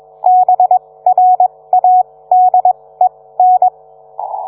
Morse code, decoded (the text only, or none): BRADEN